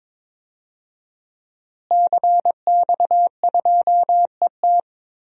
Morse code, heard as a keyed tone, about 22 wpm, 700 Hz. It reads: CX2ET
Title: DXCC-100 Call Signs - 5 Characters - Repeated 3x - 22wpm